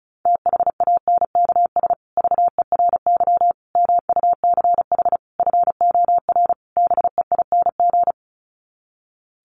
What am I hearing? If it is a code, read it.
Morse code, heard as a keyed tone, about 34 wpm, 700 Hz. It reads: THANKS VERY MUCH FOR BEING
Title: Sets of 5 Words Encore 35wpm